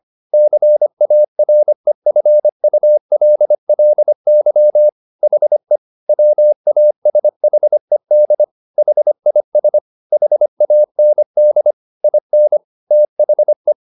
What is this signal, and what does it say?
Morse code, heard as a keyed tone, about 25 wpm, 600 Hz. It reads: CAREFULLY HE WASHED HIS HAND IN THE
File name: the old man and the sea part 1 25wpm